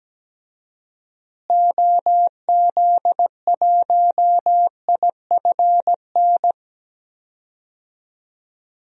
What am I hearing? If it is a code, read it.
Morse code, heard as a keyed tone, about 17 wpm, 700 Hz. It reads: OZ1IFN
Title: DXCC-100 Call Signs - 6 Characters - Repeated 3x - 17wpm